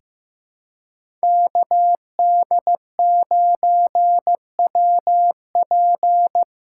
Morse code, heard as a keyed tone, about 15 wpm, 700 Hz. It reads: KD9WP
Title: USA Callsign DE Callsign - 5 Characters - 15wpm